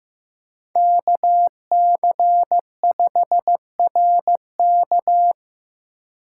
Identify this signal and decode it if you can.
Morse code, heard as a keyed tone, about 15 wpm, 700 Hz. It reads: KC5RK
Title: USA Call Signs - 5 Characters - 3x Character Spacing - 15wpm